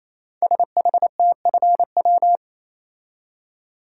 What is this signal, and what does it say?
Morse code, heard as a keyed tone, about 27 wpm, 700 Hz. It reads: SHTFW
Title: Code Groups - 5 Letters - 28wpm